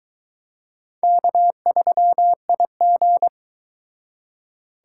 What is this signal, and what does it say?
Morse code, heard as a keyed tone, about 21 wpm, 700 Hz. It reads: K3IG